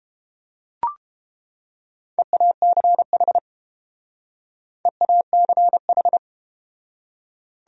Morse code, heard as a keyed tone, about 30 wpm, 700 Hz. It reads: EACH EACH